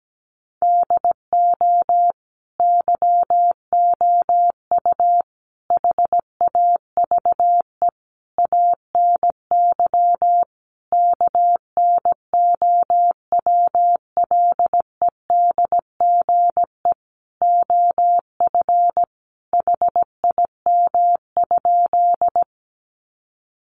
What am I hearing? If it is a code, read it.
Morse code, heard as a keyed tone, about 17 wpm, 700 Hz. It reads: DO YOU HAVE ANY KNOWLEDGE OF HIM?
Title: Sentences from Top 600 Words 17wpm